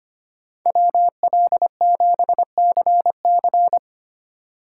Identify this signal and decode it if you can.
Morse code, heard as a keyed tone, about 25 wpm, 700 Hz. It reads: WL7CC
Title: USA Call Signs - 5 Characters - 3x Character Spacing - 25wpm